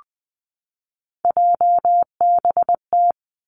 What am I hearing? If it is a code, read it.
Morse code, heard as a keyed tone, about 20 wpm, 700 Hz. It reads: JBT
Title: Three Challenging Letter-Number Combinations 20wpm